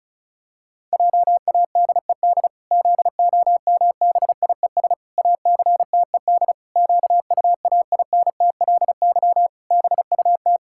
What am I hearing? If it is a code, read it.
Morse code, heard as a keyed tone, about 34 wpm, 700 Hz. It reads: JADED ZOMBIES ACTED QUAINTLY BUT
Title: Pangram Challenge v2 - 35wpm